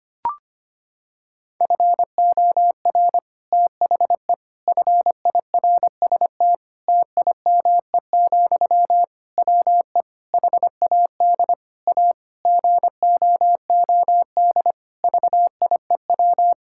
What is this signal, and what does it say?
Morse code, heard as a keyed tone, about 25 wpm, 700 Hz. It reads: FOR THE FIRST TIME, WE HAD A GOOD VIEW
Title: Sentences from Top 1000 Words - Review Part 6 - 25wpm